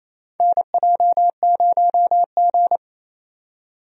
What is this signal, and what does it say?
Morse code, heard as a keyed tone, about 28 wpm, 700 Hz. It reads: NJ0G
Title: USA Call Signs - 4 Characters - 28wpm